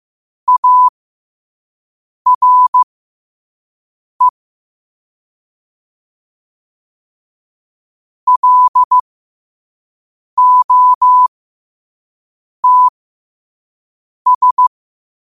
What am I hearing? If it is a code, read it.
Morse code, heard as a keyed tone, about 15 wpm, spaced as for 5 wpm, 1000 Hz. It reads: ARE LOTS
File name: qotdfiles / 5wpm